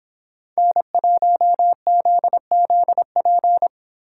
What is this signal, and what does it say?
Morse code, heard as a keyed tone, about 26 wpm, 700 Hz. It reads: N1ZZP